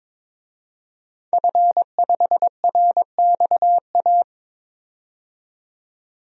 Morse code, heard as a keyed tone, about 21 wpm, 700 Hz. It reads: F5RXA
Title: DXCC-100 Call Signs - 5 Characters - 22wpm